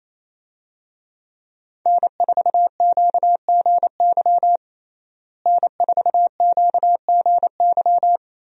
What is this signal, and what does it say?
Morse code, heard as a keyed tone, about 28 wpm, 700 Hz. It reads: N4QGY N4QGY
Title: USA Call Signs - 5 Characters - Repeated 2x - 28wpm